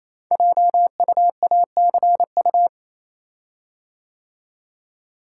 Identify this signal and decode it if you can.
Morse code, heard as a keyed tone, about 27 wpm, 700 Hz. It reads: JUACU
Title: Code Groups - 5 Letters - 28wpm